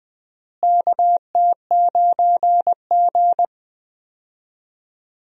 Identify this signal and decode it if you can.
Morse code, heard as a keyed tone, about 21 wpm, 700 Hz. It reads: KT9G
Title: USA Call Signs - 4 Characters - Repeated 2x - 20wpm